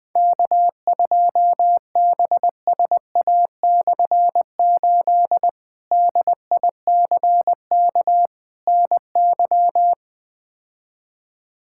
Morse code, heard as a keyed tone, about 20 wpm, 700 Hz. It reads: K2BSA/8 DICK NY
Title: K1USN SST QSO Practice - 20wpm